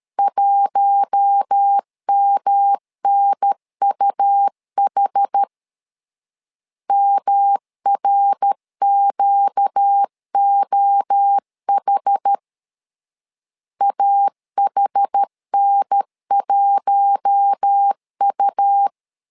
Morse code, heard as a keyed tone, about 13 wpm, 800 Hz. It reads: 1MNUH MRQOH AHN1U
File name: LEZIONI / 4 Q1 12wpm 2028